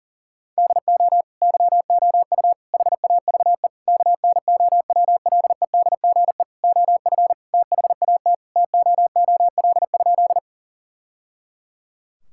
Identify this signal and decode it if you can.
Morse code, heard as a keyed tone, about 37 wpm, 700 Hz. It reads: DO YOU HAVE KNOWLEDGE OF THAT TOOL?
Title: Sentences from Top 1000 Words - Review Part 6 - 40wpm